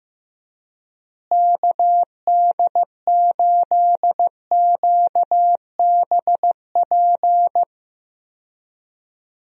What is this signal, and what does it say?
Morse code, heard as a keyed tone, about 15 wpm, 700 Hz. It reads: KD8QBP
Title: USA Call Signs - 6 Characters - Repeated 3x - 15wpm